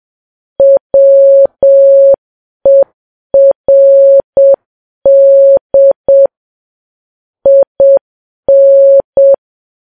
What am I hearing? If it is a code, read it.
Morse code, heard as a keyed tone, about 7 wpm, 550 Hz. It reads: WERD IN